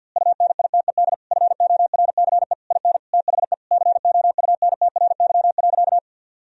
Morse code, heard as a keyed tone, about 51 wpm, 700 Hz. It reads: UNITED FORCE IN THE COUNTRY.
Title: Sentences from Top 1000 Words - Review Part 4 - 50wpm